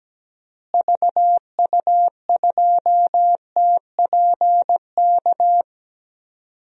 Morse code, heard as a keyed tone, about 17 wpm, 700 Hz. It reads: VU2TPK
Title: DXCC-100 Call Signs - 6 Characters - Repeated 3x - 17wpm